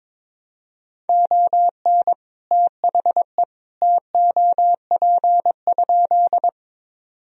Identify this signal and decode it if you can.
Morse code, heard as a keyed tone, about 22 wpm, 700 Hz. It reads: ON THE TOP?
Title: Sentences from Top 1000 Words - Review Part 2 - 22wpm